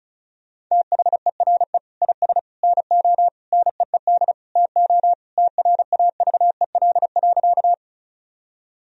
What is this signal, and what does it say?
Morse code, heard as a keyed tone, about 36 wpm, 700 Hz. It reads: THERE IS NO NEED TO TRAVEL.
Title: Sentences from Top 1000 Words - Review Part 2 - 35wpm